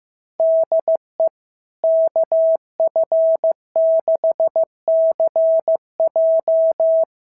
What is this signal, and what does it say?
Morse code, heard as a keyed tone, about 15 wpm, 650 Hz. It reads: DE KF6CJ
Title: SOTA QSO Practice 15wpm